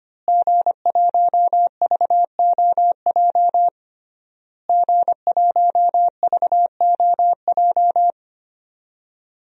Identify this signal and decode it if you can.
Morse code, heard as a keyed tone, about 25 wpm, 700 Hz. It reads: G1VOJ G1VOJ
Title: DXCC-100 Call Signs - 5 Characters - Repeated 3x - 25wpm